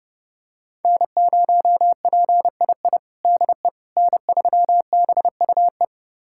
Morse code, heard as a keyed tone, about 30 wpm, 700 Hz. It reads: N0PII DE N3BUE
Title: USA Callsign DE Callsign - 5 Characters - 30wpm